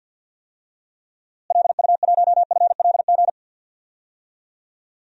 Morse code, heard as a keyed tone, about 46 wpm, 700 Hz. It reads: LU1FLG
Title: DXCC-100 Call Signs - 6 Characters - Repeated 3x - 50wpm